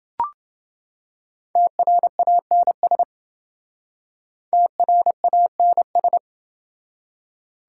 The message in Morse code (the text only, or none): TRANS TRANS